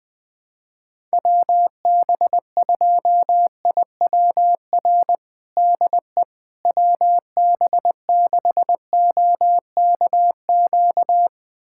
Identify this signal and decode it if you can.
Morse code, heard as a keyed tone, about 20 wpm, 700 Hz. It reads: WB2IWR DE WB6OKQ
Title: USA Callsign DE Callsign 20wpm